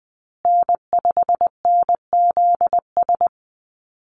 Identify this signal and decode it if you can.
Morse code, heard as a keyed tone, about 20 wpm, 700 Hz. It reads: N5NZS
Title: USA Call Signs Repeated 3x 20wpm